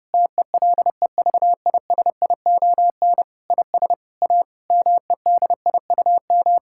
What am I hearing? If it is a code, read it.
Morse code, heard as a keyed tone, about 32 wpm, 700 Hz. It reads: TELEVISION IS A MEDIUM
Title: Sentences from Top 1000 Words - Review Part 6 - 30wpm